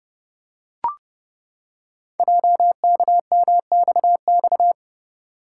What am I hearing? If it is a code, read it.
Morse code, heard as a keyed tone, about 30 wpm, 700 Hz. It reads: JKMXX
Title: Code Groups - 5 Letters - 30wpm